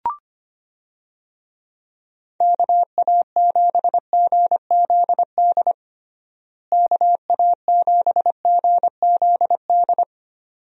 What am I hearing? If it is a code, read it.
Morse code, heard as a keyed tone, about 25 wpm, 700 Hz. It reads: KA7GZD KA7GZD
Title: USA Call Signs - 6 Characters - Repeated 2x - 25wpm